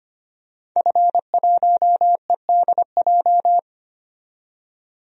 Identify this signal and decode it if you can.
Morse code, heard as a keyed tone, about 25 wpm, 700 Hz. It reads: F1EDJ
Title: DXCC-100 Call Signs - 5 Characters - 25wpm